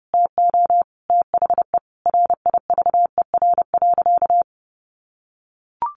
Morse code, heard as a keyed tone, about 30 wpm, 700 Hz. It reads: TO THE RIVER.
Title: Sentences from Top 1500 Words 30wpm